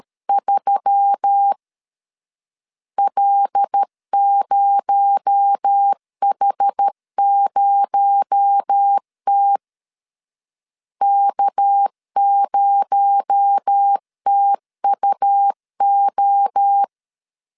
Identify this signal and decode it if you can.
Morse code, heard as a keyed tone, about 13 wpm, 800 Hz. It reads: 3 L0H0T K0TUO
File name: LEZIONI / 6 PKX0 / 6 PKX0 12wpm 2231